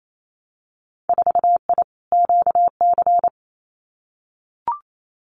Morse code, heard as a keyed tone, about 28 wpm, 700 Hz. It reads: 4I QC
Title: Winter Field Day Exchanges Encore 28wpm